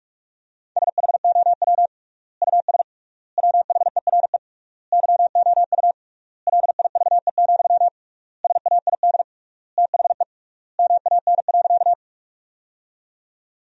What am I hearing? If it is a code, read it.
Morse code, heard as a keyed tone, about 41 wpm, 700 Hz. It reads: SHOW US WHERE YOU LIVE, SAID THE MAN.
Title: 3x Word-Spacing - Sentences from Top 200 Words - 45wpm